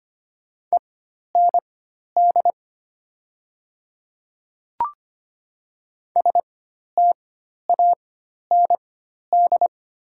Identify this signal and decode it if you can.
Morse code, heard as a keyed tone, about 25 wpm, spaced as for 12 wpm, 690 Hz. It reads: END STAND
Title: 4x Character Spacing - Top 300 Words - 25wpm